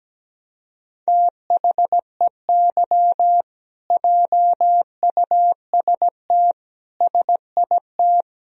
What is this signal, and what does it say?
Morse code, heard as a keyed tone, about 17 wpm, 700 Hz. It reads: THEY JUST SIT
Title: Sentences from Top 1000 Words - Review Part 3 - 17wpm